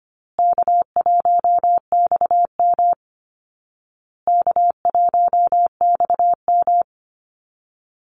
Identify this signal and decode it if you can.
Morse code, heard as a keyed tone, about 25 wpm, 700 Hz. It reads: K1XM K1XM
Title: Easy Call Signs Repeated 3x - 25wpm